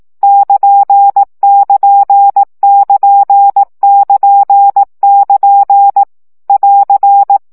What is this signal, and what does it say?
Morse code, heard as a keyed tone, about 18 wpm, 800 Hz. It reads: <KN><KN><KN><KN><KN> <AR>